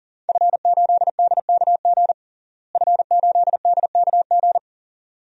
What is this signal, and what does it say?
Morse code, heard as a keyed tone, about 40 wpm, 700 Hz. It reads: F8DKG F8DKG
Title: DXCC-100 Call Signs - 5 Characters - Repeated 2x - 40wpm